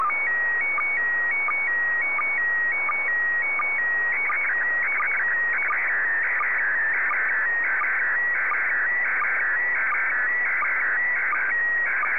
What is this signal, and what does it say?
SSTV (slow-scan television), PD 90.
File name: sstv / PD90-VK3